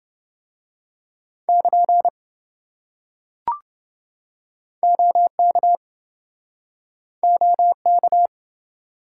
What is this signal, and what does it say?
Morse code, heard as a keyed tone, about 26 wpm, 700 Hz. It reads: <KN> OK OK